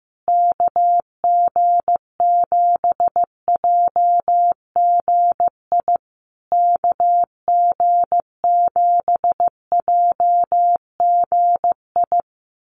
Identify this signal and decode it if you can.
Morse code, heard as a keyed tone, about 15 wpm, 700 Hz. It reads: KG7JGI KG7JGI
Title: USA Call Signs Repeated 3x 15wpm